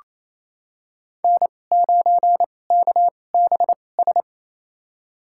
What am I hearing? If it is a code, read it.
Morse code, heard as a keyed tone, about 28 wpm, spaced as for 20 wpm, 700 Hz. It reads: N9KBS
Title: USA Call Signs - 2x Character Spacing - 28wpm